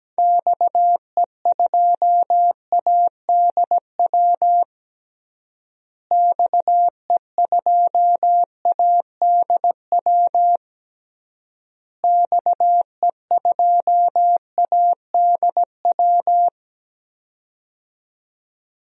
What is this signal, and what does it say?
Morse code, heard as a keyed tone, about 17 wpm, 700 Hz. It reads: XE2ADW XE2ADW XE2ADW
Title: DXCC-100 Call Signs - 6 Characters - Repeated 3x - 17wpm